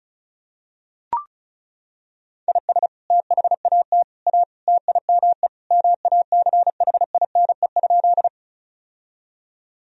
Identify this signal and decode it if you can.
Morse code, heard as a keyed tone, about 36 wpm, 700 Hz. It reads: IS THAT A TIME MACHINE?
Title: Sentences from Top 1000 Words - Review Part 2 - 35wpm